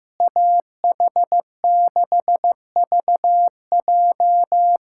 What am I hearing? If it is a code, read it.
Morse code, heard as a keyed tone, about 15 wpm, 700 Hz. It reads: AH6VJ
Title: USA Callsign DE Callsign - 5 Characters - 15wpm